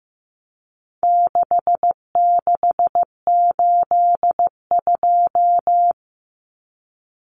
Morse code, heard as a keyed tone, about 15 wpm, 700 Hz. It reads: 6682
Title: Numbers - 4-Digit 15wpm